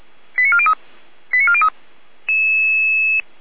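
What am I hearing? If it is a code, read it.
Five-tone selective call (ZVEI), heard as quick runs of five tones, tones of about 70 ms. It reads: 89403 89403
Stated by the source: ZVEI five-tone selective call 89402 received on an SDR.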